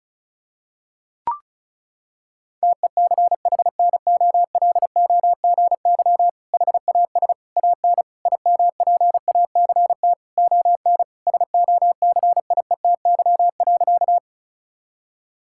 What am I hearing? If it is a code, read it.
Morse code, heard as a keyed tone, about 36 wpm, 700 Hz. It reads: TECHNOLOGY HAS AN IMPACT ON SOCIETY.
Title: Sentences from Top 1000 Words - Review Part 6 - 35wpm